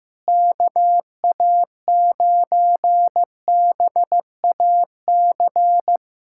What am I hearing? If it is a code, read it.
Morse code, heard as a keyed tone, about 15 wpm, 700 Hz. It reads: KA9BAC
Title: USA Callsign DE Callsign - 6 Characters - 15wpm